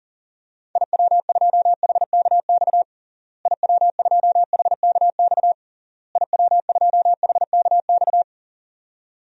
Morse code, heard as a keyed tone, about 37 wpm, 700 Hz. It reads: IW2HKX IW2HKX IW2HKX
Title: DXCC-100 Call Signs - 6 Characters - Repeated 3x - 40wpm